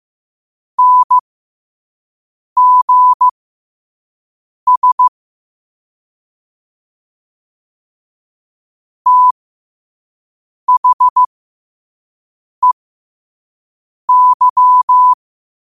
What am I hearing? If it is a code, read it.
Morse code, heard as a keyed tone, about 15 wpm, spaced as for 5 wpm, 1000 Hz. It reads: NGS THEY
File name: qotdfiles / 5wpm